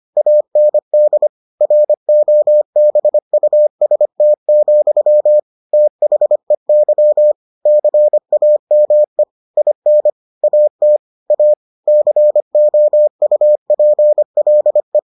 Morse code, heard as a keyed tone, about 25 wpm, 600 Hz. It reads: AND ROBUST, THEY CAME IN AT A COUPLE